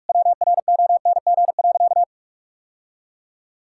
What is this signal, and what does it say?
Morse code, heard as a keyed tone, about 42 wpm, 700 Hz. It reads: WRONG.
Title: Sentences from Top 1000 Words - Review Part 5 - 45wpm